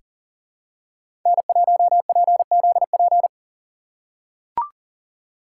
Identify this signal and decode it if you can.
Morse code, heard as a keyed tone, about 40 wpm, 700 Hz. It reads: N1PZP
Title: USA Call Signs - 5 Characters - 6x Character Spacing - 40wpm